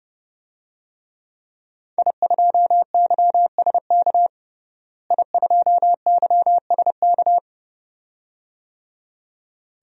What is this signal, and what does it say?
Morse code, heard as a keyed tone, about 30 wpm, 700 Hz. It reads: I2YSK I2YSK
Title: DXCC-100 Call Signs - 5 Characters - Repeated 2x - 30wpm